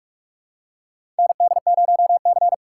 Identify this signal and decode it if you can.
Morse code, heard as a keyed tone, about 46 wpm, 700 Hz. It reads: ND0C
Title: USA Call Signs - 4 Characters - 2x Character Spacing - 45wpm